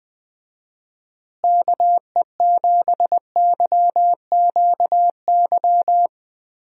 Morse code, heard as a keyed tone, about 20 wpm, 700 Hz. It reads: KE7YQY
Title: USA Call Signs - 6 Characters - Repeated 3x - 20wpm